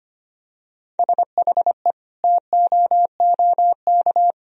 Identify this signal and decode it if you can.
Morse code, heard as a keyed tone, about 25 wpm, 700 Hz. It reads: SHE TOOK
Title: Sentences from Top 1000 Words - Review Part 2 - 25wpm